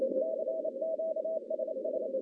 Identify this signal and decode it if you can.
Morse code, heard as a keyed tone, about 28 wpm, 640 Hz, with noise. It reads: CQ SS
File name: contests / CW-SS-k5tr- -1200